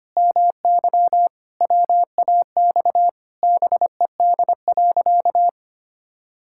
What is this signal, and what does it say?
Morse code, heard as a keyed tone, about 25 wpm, 700 Hz. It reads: MY WAX BED.
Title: Pangram Challenge v2 - 25wpm